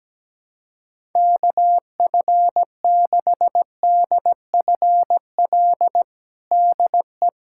Morse code, heard as a keyed tone, about 17 wpm, 700 Hz. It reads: KF6DFL DE
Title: USA Callsign DE Callsign - 6 Characters - 17wpm